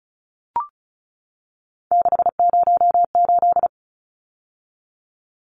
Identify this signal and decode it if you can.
Morse code, heard as a keyed tone, about 34 wpm, 700 Hz. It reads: 608
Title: Numbers - 3-Digit 35wpm